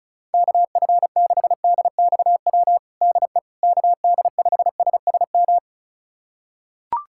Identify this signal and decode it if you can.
Morse code, heard as a keyed tone, about 36 wpm, 700 Hz. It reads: KF6DXW DE KD5SSM
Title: USA Callsign DE Callsign - 6 Characters - 35wpm